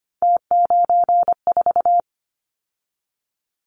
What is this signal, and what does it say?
Morse code, heard as a keyed tone, about 25 wpm, 700 Hz. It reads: T94
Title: Callsign Prefixes for Top 50 DXCC - 25wpm